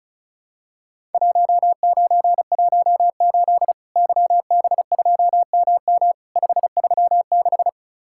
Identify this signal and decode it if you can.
Morse code, heard as a keyed tone, about 36 wpm, 700 Hz. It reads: 1918 YB2MM 536
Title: Worked All Europe DX Contest - QTC Exchange - 35wpm